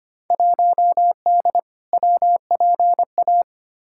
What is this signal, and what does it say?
Morse code, heard as a keyed tone, about 25 wpm, 700 Hz. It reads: 1D WPA
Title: Field Day Exchanges - 25wpm